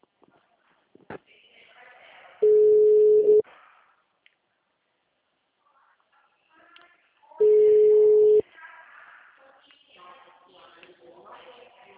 Ringback tone (Europe, 425 Hz).
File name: out- -2017- - - .26996